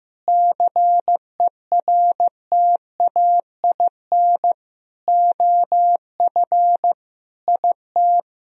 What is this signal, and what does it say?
Morse code, heard as a keyed tone, about 15 wpm, 700 Hz. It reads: CERTAIN OF IT
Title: Sentences from Top 1000 Words - Review Part 2 - 15wpm